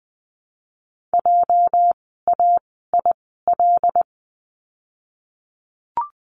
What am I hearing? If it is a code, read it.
Morse code, heard as a keyed tone, about 20 wpm, 700 Hz. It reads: J A I L
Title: 206 Lesson 33f - Letter j - Words 20wpm